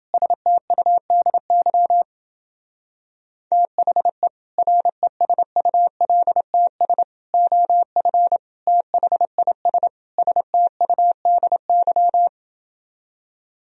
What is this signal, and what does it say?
Morse code, heard as a keyed tone, about 30 wpm, 700 Hz. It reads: STUDY THE RESULTS OF THIS STUDY